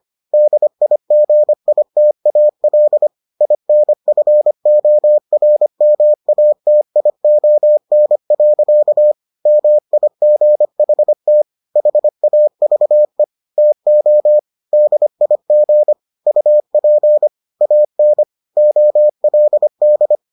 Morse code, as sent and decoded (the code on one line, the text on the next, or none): -.. .. --. .. - .- .-.. / .. -. ..-. --- .-. -- .- - .. --- -. .-.-.- / -- .. --. .... - / .... .- ...- . / - --- / -.. .. --. / ..- .--. / .- -. / --- .-.. -..
DIGITAL INFORMATION. MIGHT HAVE TO DIG UP AN OLD